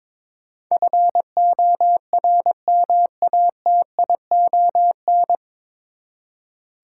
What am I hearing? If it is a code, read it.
Morse code, heard as a keyed tone, about 22 wpm, 700 Hz. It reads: FORMATION
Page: Morse Code Ninja practice 22wpm